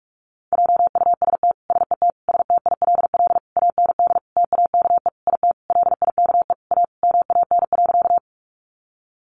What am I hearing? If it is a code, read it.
Morse code, heard as a keyed tone, about 44 wpm, 700 Hz. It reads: JUST SET STILL AND TAKE IT LIKE A MAN.
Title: Sentences from Top 300 Words 45wpm